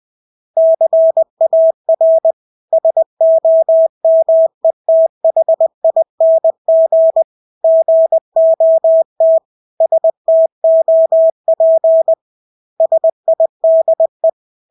Morse code, heard as a keyed tone, about 20 wpm, 650 Hz. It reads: CAR SOMETHING GOT STOP SIDE
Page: Engelske ord 20 wpm